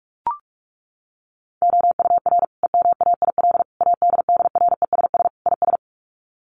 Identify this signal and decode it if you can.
Morse code, heard as a keyed tone, about 44 wpm, 700 Hz. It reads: OUR EMAIL ADDRESS IS
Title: Sets of 5 Words Encore2 45wpm